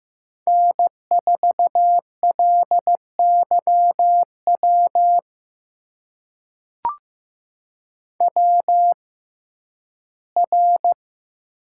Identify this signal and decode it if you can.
Morse code, heard as a keyed tone, about 15 wpm, spaced as for 11 wpm, 700 Hz. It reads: N4LYW WR
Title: USA Call Signs - 6x Character Spacing - 15wpm